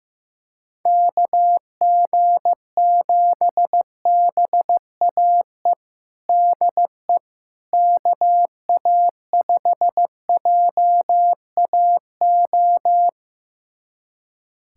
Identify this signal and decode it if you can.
Morse code, heard as a keyed tone, about 16 wpm, 700 Hz. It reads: KG7BAE DE KA5JAO
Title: USA Callsign DE Callsign - 6 Characters - 15wpm